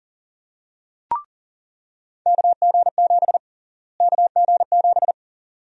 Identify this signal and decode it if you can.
Morse code, heard as a keyed tone, about 40 wpm, 700 Hz. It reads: KG7 KG7
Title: USA Call Sign Prefixes - Repeated 2x - 40wpm